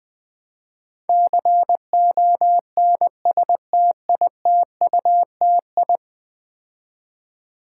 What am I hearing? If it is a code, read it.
Morse code, heard as a keyed tone, about 20 wpm, 700 Hz. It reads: CONSTITUTI
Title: Build A Word - 5 Letters At A Time - 12-Letter Words - 20wpm